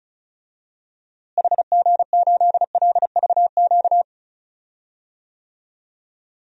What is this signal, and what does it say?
Morse code, heard as a keyed tone, about 36 wpm, 700 Hz. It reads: HG8LVQ
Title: DXCC-100 Call Signs - 6 Characters - Repeated 3x - 35wpm